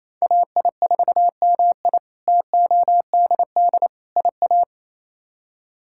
Morse code, heard as a keyed tone, about 27 wpm, 700 Hz. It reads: AI4MI TODD IA
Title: K1USN SST QSO Practice - 28wpm